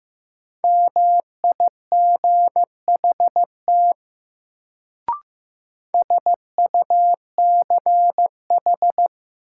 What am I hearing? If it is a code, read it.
Morse code, heard as a keyed tone, about 15 wpm, 700 Hz. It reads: MIGHT SUCH
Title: Sets of 2 Words - New Word Drill - 15wpm